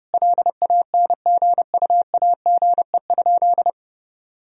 Morse code, heard as a keyed tone, about 30 wpm, 700 Hz. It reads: LANGUAGE?
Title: Sentences from Top 1000 Words - Review Part 3 - 30wpm